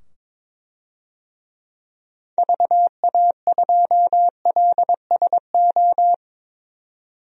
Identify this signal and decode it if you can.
Morse code, heard as a keyed tone, about 22 wpm, 700 Hz. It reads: VA2LSO
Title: DXCC-100 Call Signs - 6 Characters - 22wpm